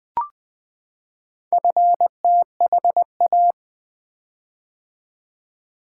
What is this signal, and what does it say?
Morse code, heard as a keyed tone, about 20 wpm, 700 Hz. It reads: FTHA
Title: Four Letter-Number Combinations - 20wpm